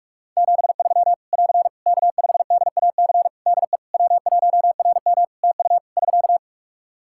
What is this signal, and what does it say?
Morse code, heard as a keyed tone, about 44 wpm, 700 Hz. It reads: 73 <AR> K5DAC DE W1RM TU <SK>
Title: Practice QSO - Exchange 4 - Ending - 45wpm